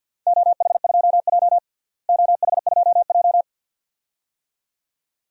Morse code, heard as a keyed tone, about 53 wpm, 700 Hz. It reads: OH2J OH2J
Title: DXCC-100 Call Signs - 4 Characters - Repeated 2x - 50wpm